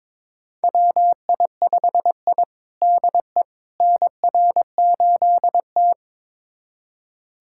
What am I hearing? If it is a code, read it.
Morse code, heard as a keyed tone, about 22 wpm, 700 Hz. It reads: WI5I DE NR8T
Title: USA Callsign DE Callsign - 4 Characters - 22wpm